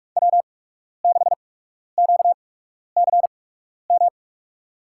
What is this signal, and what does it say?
Morse code, heard as a keyed tone, about 46 wpm, spaced as for 12 wpm, 700 Hz. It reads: W6QCM
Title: USA Call Signs - 8x Character Spacing - 45wpm